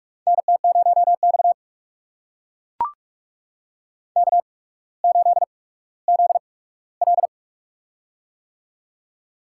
Morse code, heard as a keyed tone, about 44 wpm, spaced as for 19 wpm, 700 Hz. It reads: NT0X K8ZL
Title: USA Call Signs - 4 Characters - 8x Character Spacing - 45wpm